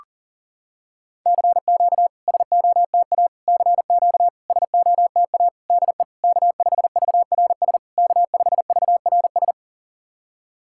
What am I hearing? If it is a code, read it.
Morse code, heard as a keyed tone, about 42 wpm, 690 Hz. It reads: CQ SOTA CQ SOTA DE K5VRS K5VRS
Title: SOTA QSO Practice 40wpm